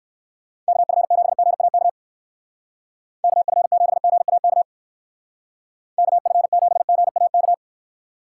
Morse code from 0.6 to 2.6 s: KF7GAK

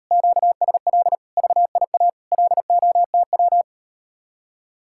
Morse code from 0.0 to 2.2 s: QSL VIA

